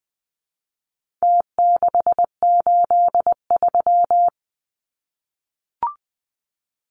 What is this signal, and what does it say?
Morse code, heard as a keyed tone, about 20 wpm, 700 Hz. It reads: T683
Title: Numbers - Contest Style - 4-Digit 20wpm